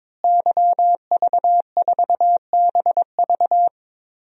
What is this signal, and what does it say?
Morse code, heard as a keyed tone, about 22 wpm, 700 Hz. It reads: YV4BV
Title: DXCC-100 Call Signs - 5 Characters - Repeated 3x - 22wpm